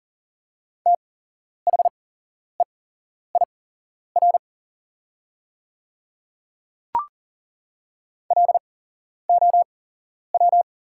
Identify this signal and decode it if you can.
Morse code, heard as a keyed tone, about 38 wpm, spaced as for 11 wpm, 700 Hz. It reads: THEIR LOW
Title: 8x Character Spacing - Top 200 Words - 40wpm